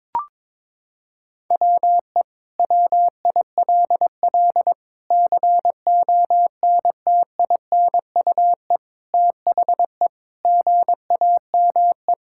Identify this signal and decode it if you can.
Morse code, heard as a keyed tone, about 22 wpm, 700 Hz. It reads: WE WILL CONTINUE THE GAME
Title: Sentences from Top 1000 Words - Review Part 6 - 22wpm